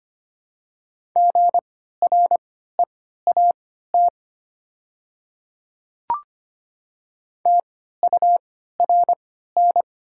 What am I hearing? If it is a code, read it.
Morse code, heard as a keyed tone, about 25 wpm, spaced as for 14 wpm, 700 Hz. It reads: GREAT TURN
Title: 3x Character Spacing - Top 300 Words - 25wpm